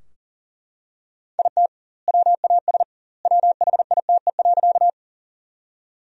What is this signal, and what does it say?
Morse code, heard as a keyed tone, about 37 wpm, 700 Hz. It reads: IT WAS WHITE.
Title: 2x Word-Spacing - Sentences from Top 300 Words - 40wpm